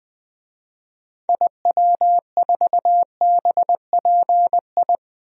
Morse code, heard as a keyed tone, about 20 wpm, 700 Hz. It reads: IW4BPI
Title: DXCC-100 Call Signs - 6 Characters - Repeated 2x - 20wpm